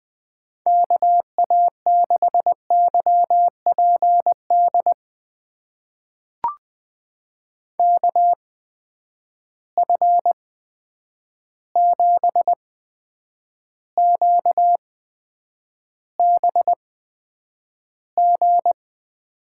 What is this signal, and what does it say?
Morse code, heard as a keyed tone, about 20 wpm, spaced as for 9 wpm, 700 Hz. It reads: KA6YPD KF7QBG
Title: USA Call Signs - 6 Characters - 8x Character Spacing - 20wpm